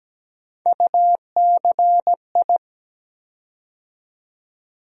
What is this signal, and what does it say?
Morse code, heard as a keyed tone, about 17 wpm, 700 Hz. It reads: UCI
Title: USA Call Sign Postfixes - 3 Letters 17wpm